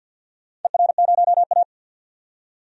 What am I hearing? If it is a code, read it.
Morse code, heard as a keyed tone, about 47 wpm, 700 Hz. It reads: ER0A
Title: DXCC-100 Call Signs - 4 Characters - 50wpm